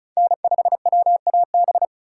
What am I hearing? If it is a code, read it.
Morse code, heard as a keyed tone, about 36 wpm, 700 Hz. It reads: N5WAB
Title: USA Call Signs - 5 Characters - 3x Character Spacing - 35wpm